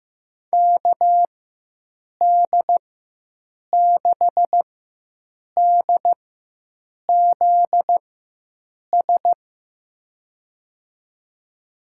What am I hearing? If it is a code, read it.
Morse code, heard as a keyed tone, about 15 wpm, spaced as for 7 wpm, 700 Hz. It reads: KD6DZS
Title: USA Call Signs - 6 Characters - 4x Character Spacing - 15wpm